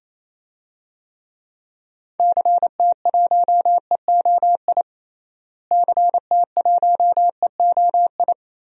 Morse code, heard as a keyed tone, about 28 wpm, 700 Hz. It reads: CT1EOI CT1EOI
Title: DXCC-100 Call Signs - 6 Characters - Repeated 3x - 28wpm